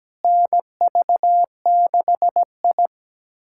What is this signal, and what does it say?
Morse code, heard as a keyed tone, about 17 wpm, 700 Hz. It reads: NV6I
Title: USA Callsign DE Callsign - 4 Characters - 17wpm